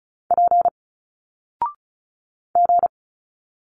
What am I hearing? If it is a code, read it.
Morse code, heard as a keyed tone, about 35 wpm, 700 Hz. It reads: P G
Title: Single Letters 35wpm